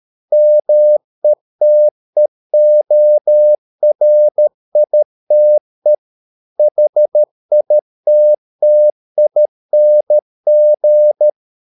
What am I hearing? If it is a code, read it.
Morse code, heard as a keyed tone, about 13 wpm, 600 Hz. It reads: METEORITE HITTING